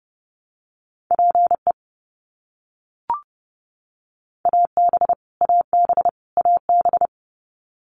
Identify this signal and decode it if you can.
Morse code, heard as a keyed tone, about 30 wpm, 700 Hz. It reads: PE AB AB AB
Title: Canadian Provinces and Territories - Repeated 3x - 30wpm